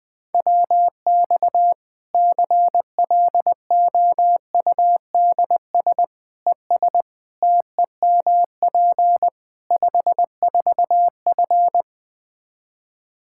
Morse code, heard as a keyed tone, about 20 wpm, 700 Hz. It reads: WX CLOUDS ES TEMP 54F
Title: Practice QSO - Exchange 2 - Info - 20wpm